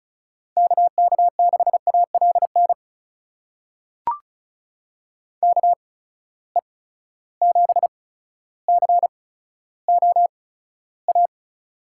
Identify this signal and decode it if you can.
Morse code, heard as a keyed tone, about 36 wpm, spaced as for 15 wpm, 700 Hz. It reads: KK6ALN KE7COA
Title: USA Call Signs - 6 Characters - 8x Character Spacing - 35wpm